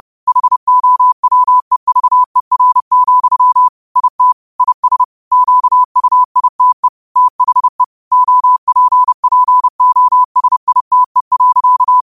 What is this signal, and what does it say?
Morse code, heard as a keyed tone, about 30 wpm, 1000 Hz. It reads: HOWEVER, IT IS QUITE THE OPPOSITE.
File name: qotdfiles / 30wpm